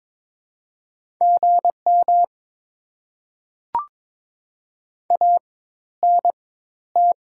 Morse code, heard as a keyed tone, about 22 wpm, spaced as for 12 wpm, 690 Hz. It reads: GM ANT
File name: Top 20 QSO Elements - 4x Character Spacing - 22wpm